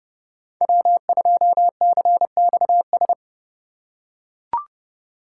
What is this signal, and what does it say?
Morse code, heard as a keyed tone, about 29 wpm, 700 Hz. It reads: W2CXS
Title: Code Groups - 5 Letter:Numbers - 30wpm